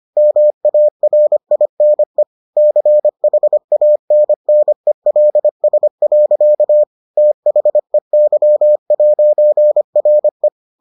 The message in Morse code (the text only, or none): MARINE CHANNELS. THEY'RE